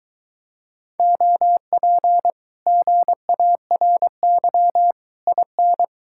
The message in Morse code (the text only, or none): OP GARY IN